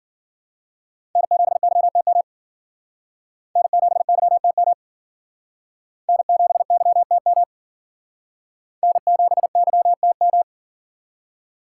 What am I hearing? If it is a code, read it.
Morse code, heard as a keyed tone, about 52 wpm, 700 Hz. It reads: N7YTM N7YTM N7YTM N7YTM